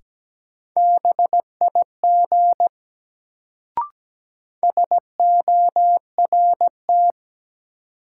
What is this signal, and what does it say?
Morse code, heard as a keyed tone, about 17 wpm, 700 Hz. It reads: BIG SORT
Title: Sets of 2 Words - New Word Drill - 17wpm